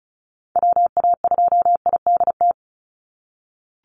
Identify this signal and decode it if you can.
Morse code, heard as a keyed tone, about 34 wpm, 700 Hz. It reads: WA2IDT
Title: USA Call Signs 35wpm